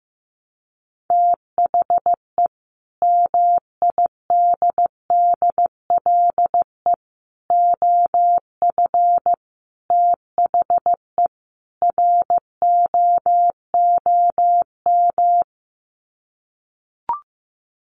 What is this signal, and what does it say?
Morse code, heard as a keyed tone, about 15 wpm, 700 Hz. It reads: THE MIDDLE OF THE ROOM
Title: Sets of 5 Words Encore2 15wpm